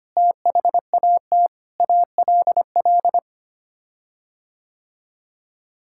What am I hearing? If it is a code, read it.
Morse code, heard as a keyed tone, about 25 wpm, 700 Hz. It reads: THAT ALL
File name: Sets of 9 Words - 25wpm